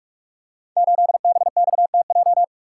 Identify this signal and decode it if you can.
Morse code, heard as a keyed tone, about 44 wpm, 700 Hz. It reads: 8BXTJ
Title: Code Groups - 5 Letter:Numbers - 45wpm